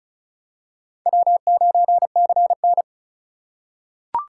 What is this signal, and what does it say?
Morse code, heard as a keyed tone, about 36 wpm, 700 Hz. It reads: W9CN
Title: USA Call Signs - 4 Characters - 3x Character Spacing - 35wpm